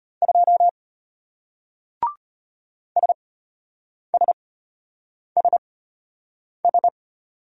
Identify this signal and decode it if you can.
Morse code, heard as a keyed tone, about 33 wpm, spaced as for 8 wpm, 700 Hz. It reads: 2 SSSS